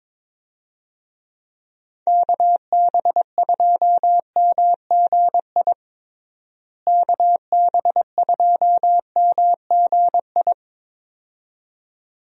Morse code, heard as a keyed tone, about 22 wpm, 700 Hz. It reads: KB2MGI KB2MGI
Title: USA Call Signs - 6 Characters - Repeated 2x - 22wpm